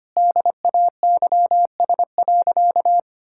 Morse code, heard as a keyed tone, about 25 wpm, 700 Hz. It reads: DAYS.
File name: practice / 25wpm / Sentences with 3-Word Phrases - 25wpm